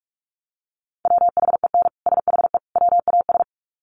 Morse code, heard as a keyed tone, about 44 wpm, 700 Hz. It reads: WHEN SHE WAS
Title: Sets of 3 Words 45wpm